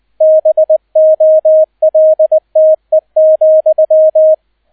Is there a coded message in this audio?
Yes: Morse code from 0.2 to 4.3 s, heard as a keyed tone, about 20 wpm, 630 Hz, with radio noise. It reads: BOLTE,